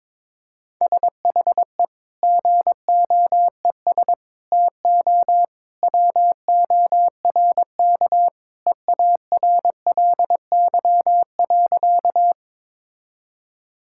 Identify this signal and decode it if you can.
Morse code, heard as a keyed tone, about 22 wpm, 700 Hz. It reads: SHE GOES TO WORK EARLY.
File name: practice / 22wpm / Sentences with 2-Word Phrases - 22wpm